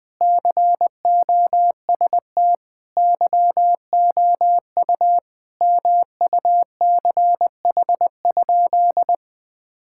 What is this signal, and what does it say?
Morse code, heard as a keyed tone, about 21 wpm, 700 Hz. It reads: COST YOU MUCH?
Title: Sentences from Top 1000 Words - Review Part 6 - 20wpm